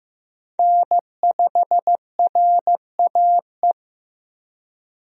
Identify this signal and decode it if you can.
Morse code, heard as a keyed tone, about 15 wpm, 700 Hz. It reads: N5RAE
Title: USA Call Signs - 5 Characters - Repeated 3x - 15wpm